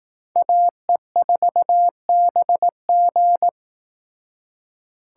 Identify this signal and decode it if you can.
Morse code, heard as a keyed tone, about 18 wpm, 700 Hz. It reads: AE4BG
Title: Speed Racing - USA 2x2 Call Signs 15wpm